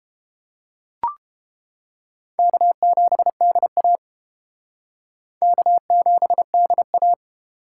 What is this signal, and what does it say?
Morse code, heard as a keyed tone, about 29 wpm, 700 Hz. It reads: K7DA K7DA